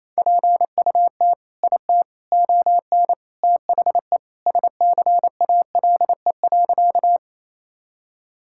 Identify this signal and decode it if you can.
Morse code, heard as a keyed tone, about 28 wpm, 700 Hz. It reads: PUT IT ON THE SCALE.
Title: Sentences from Top 1000 Words - Review Part 5 - 28wpm